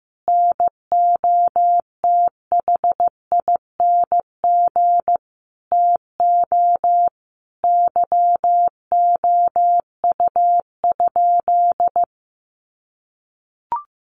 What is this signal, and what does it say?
Morse code, heard as a keyed tone, about 15 wpm, 700 Hz. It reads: NOTHING TO YOU?
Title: Sentences from Top 1300 Words 15wpm